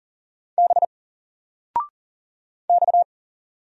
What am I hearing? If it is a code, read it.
Morse code, heard as a keyed tone, about 39 wpm, 700 Hz. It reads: B X
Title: Single Character (Letter:Number:Punctuation:Prosign) 40wpm